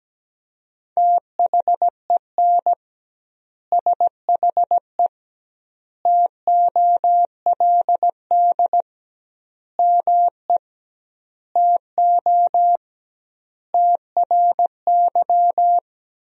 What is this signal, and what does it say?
Morse code, heard as a keyed tone, about 17 wpm, 700 Hz. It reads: THEN SHE TOLD ME TO TRY
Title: 2x Word-Spacing - Sentences from Top 400 Words - 17wpm